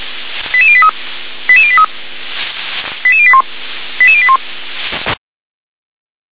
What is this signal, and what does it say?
Five-tone selective call (ZVEI), heard as quick runs of five tones, tones of about 70 ms. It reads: 88083 88083 88081 88081
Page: /357/SDR/ZVEI/